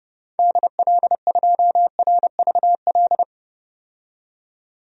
Morse code, heard as a keyed tone, about 30 wpm, 700 Hz. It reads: DL2RVL
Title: DXCC-100 Call Signs - 6 Characters - 30wpm